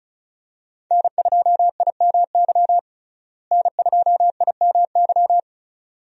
Morse code, heard as a keyed tone, about 36 wpm, 700 Hz. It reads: N2IMY N2IMY
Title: USA Call Signs - 5 Characters - Repeated 2x - 35wpm